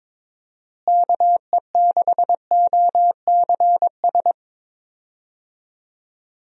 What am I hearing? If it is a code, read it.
Morse code, heard as a keyed tone, about 22 wpm, 700 Hz. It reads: KE6OCS
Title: USA Call Signs - 6 Characters - Repeated 3x - 22wpm